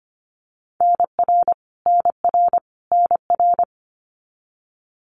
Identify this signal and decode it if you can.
Morse code, heard as a keyed tone, about 25 wpm, 700 Hz. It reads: NR NR NR
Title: Abbreviations Repeated 3x - 25wpm